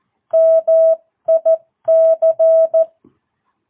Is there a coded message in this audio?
Yes: Morse code from 0.3 to 2.8 s, heard as a keyed tone, about 14 wpm, 650 Hz. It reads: MIC